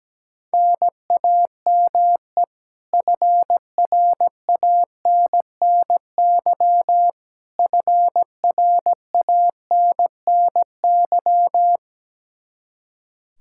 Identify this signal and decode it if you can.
Morse code, heard as a keyed tone, about 17 wpm, 700 Hz. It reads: NAME FRANNY FRANNY
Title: Practice QSO - Exchange 1 - Beginning - 17wpm